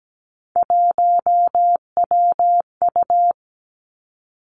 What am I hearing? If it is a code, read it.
Morse code, heard as a keyed tone, about 17 wpm, 700 Hz. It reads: 1WU
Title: Three Challenging Letter-Number Combinations 17wpm